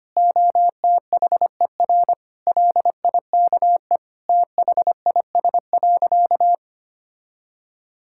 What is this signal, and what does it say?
Morse code, heard as a keyed tone, about 25 wpm, 700 Hz. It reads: OTHER LIKE THIS.
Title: Sentences from Top 1000 Words - Review Part 6 - 25wpm